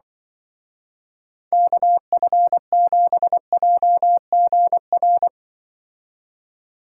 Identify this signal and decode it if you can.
Morse code, heard as a keyed tone, about 25 wpm, 700 Hz. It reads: KF7JGR
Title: Speed Racing - USA 2x3 Call Signs 20wpm